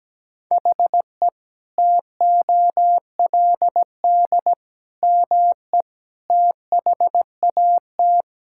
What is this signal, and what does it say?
Morse code, heard as a keyed tone, about 17 wpm, 700 Hz. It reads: HE TOLD ME THAT
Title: Sentences from Top 1600 Words 17wpm